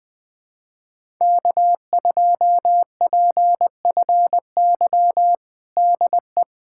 Morse code, heard as a keyed tone, about 20 wpm, 700 Hz. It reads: K2PFY DE
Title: USA Callsign DE Callsign 20wpm